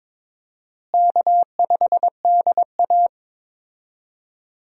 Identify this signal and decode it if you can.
Morse code, heard as a keyed tone, about 21 wpm, 700 Hz. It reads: K5DA
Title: USA Call Signs - 4 Characters - Repeated 2x - 22wpm